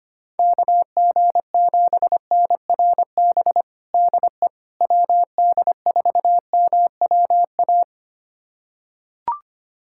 Morse code, heard as a keyed tone, about 25 wpm, 700 Hz. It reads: KG7NRB DE WD4MWA
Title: USA Callsign DE Callsign 25wpm